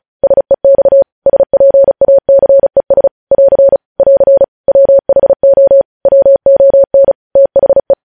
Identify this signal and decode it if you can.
Morse code, heard as a keyed tone, about 36 wpm, 550 Hz. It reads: SEX SPACES <AR> <AR> WHO WON THE
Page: News headlines in Morse code at 35 wpm